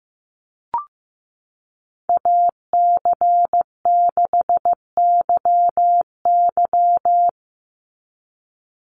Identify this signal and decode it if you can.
Morse code, heard as a keyed tone, about 15 wpm, 700 Hz. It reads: AC6YY
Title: Call Signs Repeated 2x - 15wpm